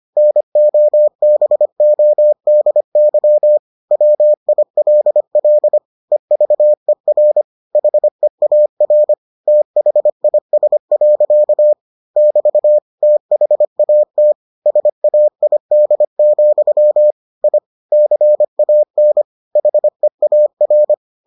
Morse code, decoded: NOBODY WILL EVER HEAR THIS. <BT> THAT SAID, I CAN HEAR

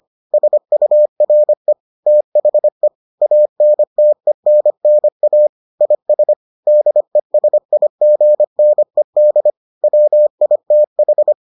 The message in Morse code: SURE THE ANTENNA IS DESIGNED WITH

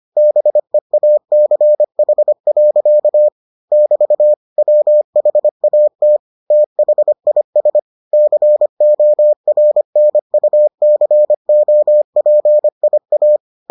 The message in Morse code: BEACH. <BT> WHAT THIS CORNUCOPIA

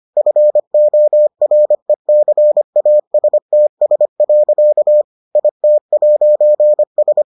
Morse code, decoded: FORECASTS. IT'S

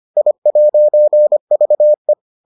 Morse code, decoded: I'VE

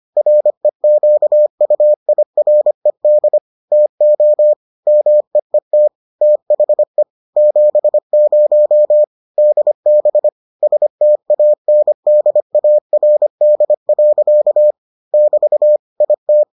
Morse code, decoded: REQUIRED TO MEET THE 70 DB STANDARD. <BT> IT

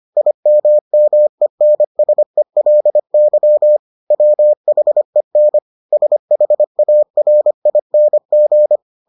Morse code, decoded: IMMENSELY WHEN SHARING

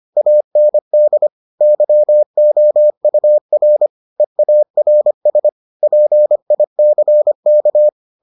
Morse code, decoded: AND YOUR EARS PICK

